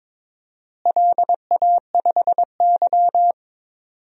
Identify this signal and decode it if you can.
Morse code, heard as a keyed tone, about 22 wpm, 700 Hz. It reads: LA5Y
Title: DXCC-100 Call Signs - 4 Characters - Repeated 3x - 22wpm